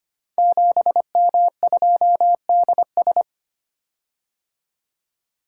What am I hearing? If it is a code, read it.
Morse code, heard as a keyed tone, about 25 wpm, 700 Hz. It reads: 7M2DS
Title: Code Groups - 5 Letter:Numbers - 25wpm